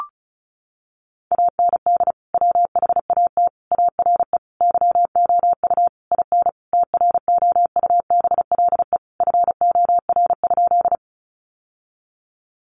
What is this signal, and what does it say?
Morse code, heard as a keyed tone, about 34 wpm, 700 Hz. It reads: AND WHAT ARE YOU IN TROUBLE FOR?
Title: Sentences from Top 1200 Words 35wpm